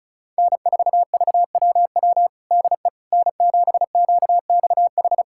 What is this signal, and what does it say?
Morse code, heard as a keyed tone, about 36 wpm, 700 Hz. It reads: N4VWW DE N7QXH
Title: USA Callsign DE Callsign - 5 Characters - 35wpm